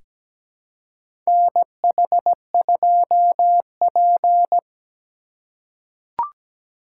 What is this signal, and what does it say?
Morse code, heard as a keyed tone, about 17 wpm, 700 Hz. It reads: NH2P
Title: USA Call Signs - 4 Characters - 2x Character Spacing - 17wpm